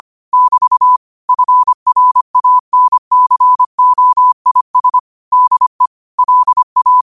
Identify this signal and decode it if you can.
Morse code, heard as a keyed tone, about 25 wpm, 1000 Hz. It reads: <BT> FRANCOIS DE LA